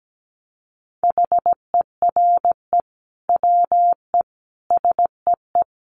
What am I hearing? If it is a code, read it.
Morse code, heard as a keyed tone, about 17 wpm, 700 Hz. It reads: HERE WE SEE
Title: Sentences from Top 300 Words 17wpm